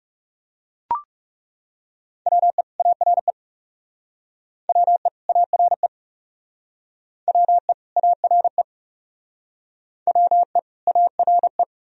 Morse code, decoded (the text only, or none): WE ARE WE ARE WE ARE WE ARE